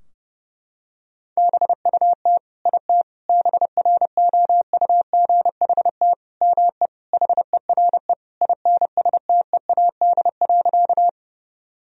Morse code, heard as a keyed tone, about 30 wpm, 700 Hz. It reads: BUT IT BROUGHT ME HERE INSTEAD.
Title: Sentences from Top 1000 Words - Review Part 4 - 30wpm